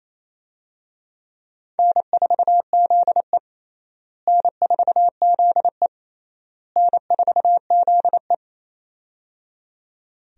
Morse code, heard as a keyed tone, about 27 wpm, 700 Hz. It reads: N4ZE N4ZE N4ZE
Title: USA Call Signs - 4 Characters - Repeated 3x - 28wpm